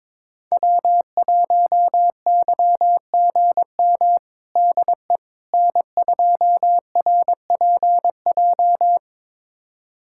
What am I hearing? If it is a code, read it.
Morse code, heard as a keyed tone, about 22 wpm, 700 Hz. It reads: W1YGM DE N2RPJ
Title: USA Callsign DE Callsign - 5 Characters - 22wpm